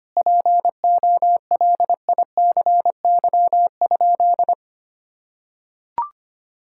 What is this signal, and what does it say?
Morse code, heard as a keyed tone, about 25 wpm, 700 Hz. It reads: POLICY?
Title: Sentences from Top 1000 Words - Review Part 5 - 25wpm